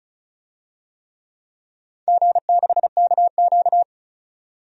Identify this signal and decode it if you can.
Morse code, heard as a keyed tone, about 36 wpm, 690 Hz. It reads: G6KQ
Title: DXCC-100 Call Signs - 4 Characters - 35wpm